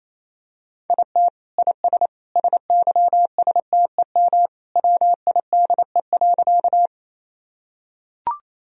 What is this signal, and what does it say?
Morse code, heard as a keyed tone, about 28 wpm, 700 Hz. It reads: IT IS SYSTEM WIDE.
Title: Sentences from Top 1000 Words - Review Part 3 - 28wpm